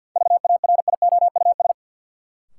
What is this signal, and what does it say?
Morse code, heard as a keyed tone, about 53 wpm, 700 Hz. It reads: VARIOUS
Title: Sentences from Top 1000 Words - Review Part 4 - New Word Drill - 50wpm